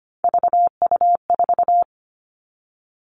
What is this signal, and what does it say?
Morse code, heard as a keyed tone, about 25 wpm, 700 Hz. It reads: VU4
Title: All Cut Numbers - 3-Digit 25wpm